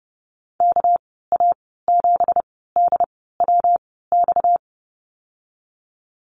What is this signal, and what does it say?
Morse code, heard as a keyed tone, about 30 wpm, spaced as for 17 wpm, 700 Hz. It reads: KA7DWX
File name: Lesson 37g - Letter x - Call Signs 30wpm